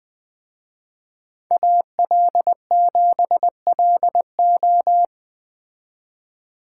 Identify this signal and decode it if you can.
Morse code, heard as a keyed tone, about 20 wpm, 700 Hz. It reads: AL7LO
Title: Speed Racing - USA 2x2 Call Signs 17wpm